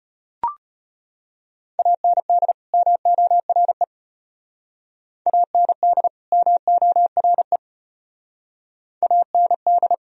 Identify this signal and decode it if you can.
Morse code, heard as a keyed tone, about 35 wpm, 700 Hz. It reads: AND MORE AND MORE AND